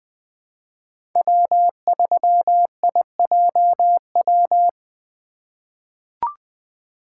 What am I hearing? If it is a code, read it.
Morse code, heard as a keyed tone, about 20 wpm, 700 Hz. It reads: W3IJW
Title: USA Call Signs - 5 Characters - 2x Character Spacing - 20wpm